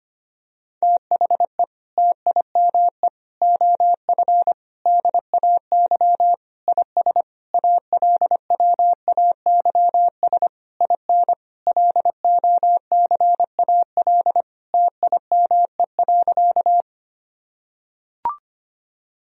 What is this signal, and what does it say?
Morse code, heard as a keyed tone, about 25 wpm, 700 Hz. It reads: THE TIME OF DAY IS ALWAYS IN LOCAL TIME.
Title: Sentences from Top 1000 Words - Review Part 3 - 25wpm